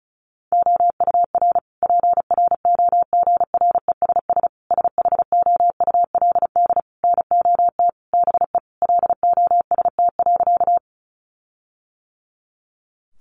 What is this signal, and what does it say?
Morse code, heard as a keyed tone, about 34 wpm, 700 Hz. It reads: OUR PROGRESS SHOULD NOT BE LOST.
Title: Sentences from Top 1400 Words 35wpm